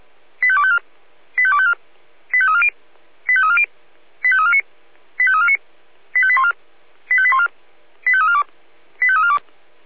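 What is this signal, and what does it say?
Five-tone selective call (ZVEI), heard as quick runs of five tones, tones of about 70 ms. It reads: 85254 85254 85340 85340 85349 85349 85814 85814 85342 85342
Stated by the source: Five-tone ZVEI selective call 85350 received by SDR.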